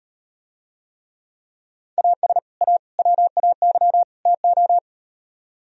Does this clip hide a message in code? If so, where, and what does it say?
Yes: Morse code from 2.0 to 4.8 s, heard as a keyed tone, about 38 wpm, 700 Hz. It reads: AS A WAY TO